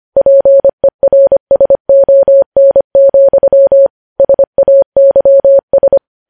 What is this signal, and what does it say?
Morse code, heard as a keyed tone, about 25 wpm, 550 Hz. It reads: PERSON, SAYS
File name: podcast / News headlines-1x-25x25